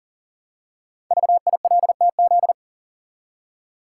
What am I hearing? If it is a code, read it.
Morse code, heard as a keyed tone, about 41 wpm, 700 Hz. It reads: VILTZ
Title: Code Groups - 5 Letters - 40wpm